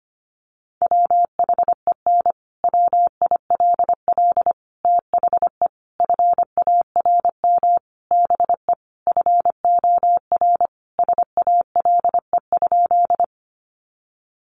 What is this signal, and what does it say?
Morse code, heard as a keyed tone, about 25 wpm, 700 Hz. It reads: WHEN WILL THE FARM BE FOR SALE?
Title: Sentences from Top 1400 Words 25wpm